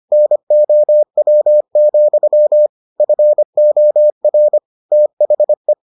Morse code, heard as a keyed tone, about 25 wpm, 600 Hz. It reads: NOW, FOR THE